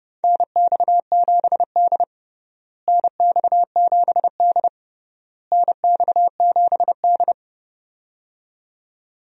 Morse code, heard as a keyed tone, about 30 wpm, 700 Hz. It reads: NX7D NX7D NX7D
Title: USA Call Signs - 4 Characters - Repeated 3x - 30wpm